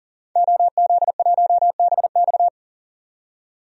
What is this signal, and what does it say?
Morse code, heard as a keyed tone, about 40 wpm, 700 Hz. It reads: OZ1BX
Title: DXCC-100 Call Signs - 5 Characters - 40wpm